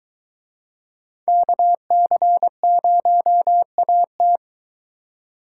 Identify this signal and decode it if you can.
Morse code, heard as a keyed tone, about 23 wpm, 700 Hz. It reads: KC0AT